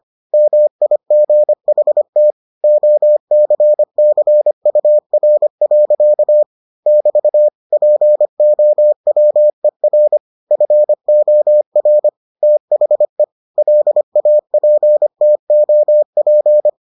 Morse code, heard as a keyed tone, about 25 wpm, 600 Hz. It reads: MIGHT OCCUR. <BT> POWER FOR THE LAPTOP